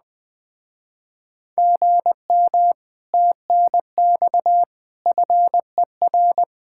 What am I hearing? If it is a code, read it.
Morse code, heard as a keyed tone, about 20 wpm, 700 Hz. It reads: GM TNX FER
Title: Speed-Racing Common QSO Phrases Encore - 17wpm